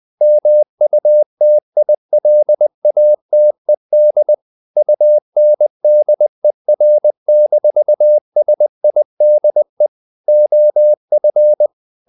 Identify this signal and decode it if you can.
Morse code, heard as a keyed tone, about 20 wpm, 600 Hz. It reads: MUTILATED UNDER-SIDE OF
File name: the old man and the sea part 2 20wpm